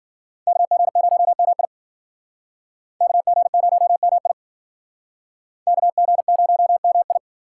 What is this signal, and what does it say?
Morse code, heard as a keyed tone, about 51 wpm, 700 Hz. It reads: KG0MI KG0MI KG0MI